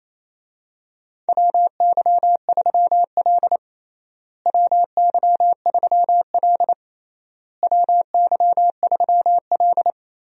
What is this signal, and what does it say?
Morse code, heard as a keyed tone, about 28 wpm, 700 Hz. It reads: WY3L WY3L WY3L
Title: USA Call Signs - 4 Characters - Repeated 3x - 28wpm